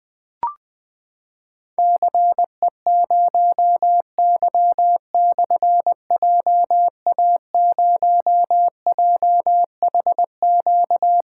Morse code, heard as a keyed tone, about 20 wpm, 700 Hz. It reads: CE0Y/JA0JHQ
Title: Difficult Call Signs 20wpm